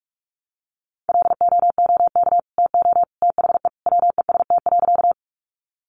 Morse code, heard as a keyed tone, about 47 wpm, 700 Hz. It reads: LOOK TO THE WEST.
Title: Sentences from Top 400 Words 45wpm